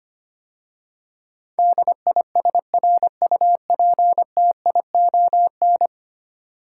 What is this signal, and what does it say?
Morse code, heard as a keyed tone, about 25 wpm, 690 Hz. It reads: DISRUPTION
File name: 10 Letter Words - First 3 Letters Given - 25wpm